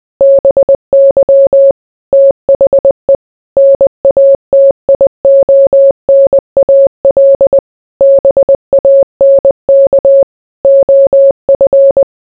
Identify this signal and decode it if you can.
Morse code, heard as a keyed tone, about 20 wpm, 550 Hz. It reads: BY THE NATIONAL BANK OF